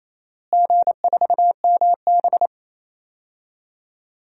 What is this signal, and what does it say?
Morse code, heard as a keyed tone, about 27 wpm, 700 Hz. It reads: G4MB
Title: DXCC-100 Call Signs - 4 Characters - Repeated 2x - 28wpm